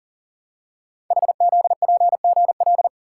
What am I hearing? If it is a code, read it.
Morse code, heard as a keyed tone, about 39 wpm, 690 Hz. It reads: HZPGL
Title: Code Groups - 5 Letters - 40wpm